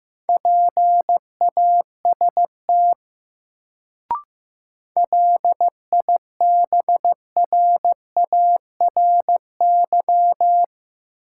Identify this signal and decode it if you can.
Morse code, heard as a keyed tone, about 15 wpm, 700 Hz. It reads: PAST LIBRARY
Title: Sentences from Top 600 Words - New Word Drill - 15wpm